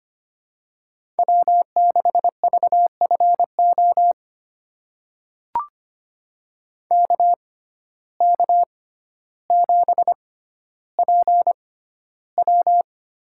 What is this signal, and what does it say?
Morse code, heard as a keyed tone, about 25 wpm, spaced as for 13 wpm, 700 Hz. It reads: W6VFO KK7PW
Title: USA Call Signs - 5 Characters - 6x Character Spacing - 25wpm